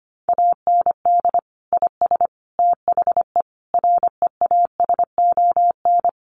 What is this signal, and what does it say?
Morse code, heard as a keyed tone, about 25 wpm, 700 Hz. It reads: AND IS THE REASON
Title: Sentences from Top 700 Words 25wpm